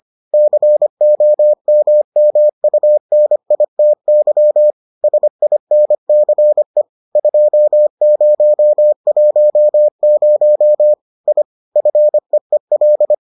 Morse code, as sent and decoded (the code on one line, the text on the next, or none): -.-. --- -- -- ..- -. .. - -.-- / ... .. -. -.-. / ..--- ----- .---- ----- / .. / ..-. . .-..
COMMUNITY SINCE 2010 I FEEL